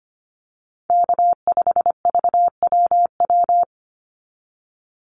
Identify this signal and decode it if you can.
Morse code, heard as a keyed tone, about 25 wpm, 700 Hz. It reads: K5VWW
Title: Call Signs Repeated 2x - 25wpm